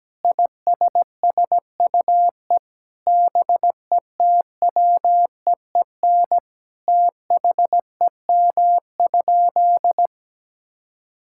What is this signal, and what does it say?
Morse code, heard as a keyed tone, about 17 wpm, 700 Hz. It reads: ISSUE BETWEEN THEM?
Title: Sentences from Top 1000 Words - Review Part 6 - 17wpm